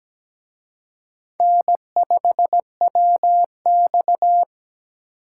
Morse code, heard as a keyed tone, about 20 wpm, 700 Hz. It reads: N5WX
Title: Speed Racing - USA Call Signs 17wpm